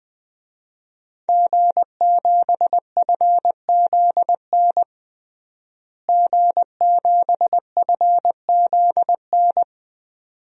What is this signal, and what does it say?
Morse code, heard as a keyed tone, about 20 wpm, 700 Hz. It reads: G7FZN G7FZN
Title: DXCC-100 Call Signs - 5 Characters - Repeated 2x - 20wpm